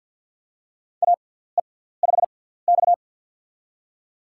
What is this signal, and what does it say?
Morse code, heard as a keyed tone, about 51 wpm, spaced as for 17 wpm, 700 Hz. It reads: AE5X
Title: USA Call Signs - 4 Characters - 6x Character Spacing - 50wpm